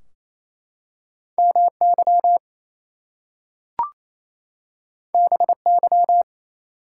Morse code, heard as a keyed tone, about 27 wpm, 700 Hz. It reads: MY BY
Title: Two Letter Words - 28wpm